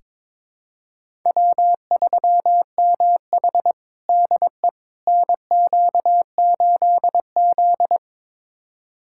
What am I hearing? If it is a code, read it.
Morse code, heard as a keyed tone, about 22 wpm, 700 Hz. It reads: W3MH DE NQ8Z
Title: USA Callsign DE Callsign - 4 Characters - 22wpm